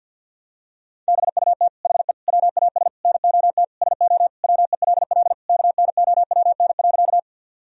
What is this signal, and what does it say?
Morse code, heard as a keyed tone, about 52 wpm, 700 Hz. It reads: BUT HE WAS NOT SO WELL KNOWN.
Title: Sentences from Top 1000 Words - Review Part 3 - 50wpm